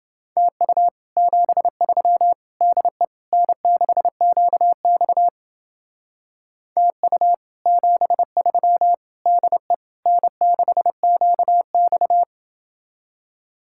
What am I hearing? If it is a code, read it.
Morse code, heard as a keyed tone, about 30 wpm, 700 Hz. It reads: TU 73 DE N6QX TU 73 DE N6QX